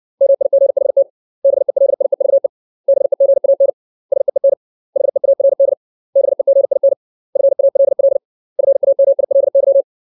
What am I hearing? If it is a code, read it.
Morse code, heard as a keyed tone, about 60 wpm, 550 Hz. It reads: MIGHT BELIEVE BEGAN SEEN HEARD BEGIN FALL FAMILY
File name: 632 words / 632 Words 60WPM